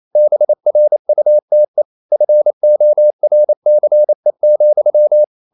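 Morse code, heard as a keyed tone, about 28 wpm, 600 Hz. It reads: BRUTE FORCE,
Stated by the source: Morse code practice at 28 wpm, text of Ken Follett's novel Never in English, from CW med Gnister.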